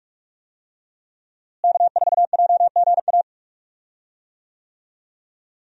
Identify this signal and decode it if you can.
Morse code, heard as a keyed tone, about 45 wpm, 700 Hz. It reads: K4JGA
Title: USA Call Signs - 5 Characters - Repeated 3x - 45wpm